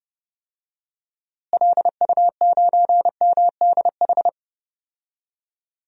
Morse code, heard as a keyed tone, about 30 wpm, 700 Hz. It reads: LU9MDH
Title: DXCC-100 Call Signs - 6 Characters - Repeated 2x - 30wpm